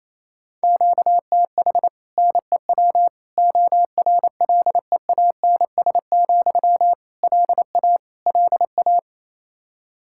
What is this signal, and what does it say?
Morse code, heard as a keyed tone, about 28 wpm, 700 Hz. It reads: QTH NEW ORLEANS, LA LA
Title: Practice QSO - Exchange 1 - Beginning - 28wpm